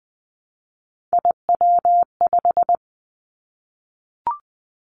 Morse code, heard as a keyed tone, about 20 wpm, 700 Hz. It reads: IW5
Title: Three Challenging Letter-Number Combinations 20wpm